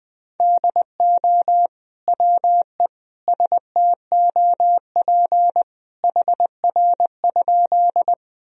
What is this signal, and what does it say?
Morse code, heard as a keyed tone, about 21 wpm, 700 Hz. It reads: DO WE STOP HR?
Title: Sentences from Top 300 Words and Abbreviations 20wpm